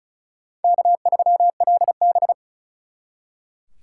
Morse code, heard as a keyed tone, about 45 wpm, 700 Hz. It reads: K3LB